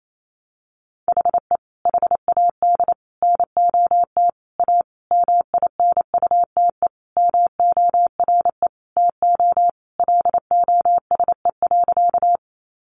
Morse code, heard as a keyed tone, about 28 wpm, 700 Hz. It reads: HE HAD NOT A MINUTE MORE TO LOSE.
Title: Sentences from Top 1100 Words 28wpm